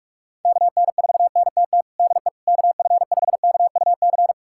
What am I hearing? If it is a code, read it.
Morse code, heard as a keyed tone, about 41 wpm, 700 Hz. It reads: KN4NTT DE KF5KUC
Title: USA Callsign DE Callsign 45wpm